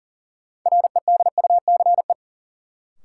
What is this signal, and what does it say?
Morse code, heard as a keyed tone, about 40 wpm, 700 Hz. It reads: REDUCE
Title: Sentences from Top 1100 Words - New Word Drill Alt - 40wpm